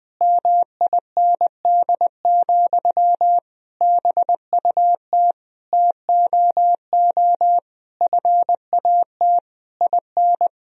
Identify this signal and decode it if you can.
Morse code, heard as a keyed tone, about 21 wpm, 700 Hz. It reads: MIND, BUT TOO FAT IN
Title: Sentences from Top 1000 Words - Review Part 4 - 20wpm